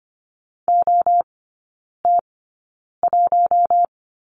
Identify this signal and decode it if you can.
Morse code, heard as a keyed tone, about 25 wpm, spaced as for 9 wpm, 700 Hz. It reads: OT1
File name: Lesson 14e - Number 2 - Three Characters 25wpm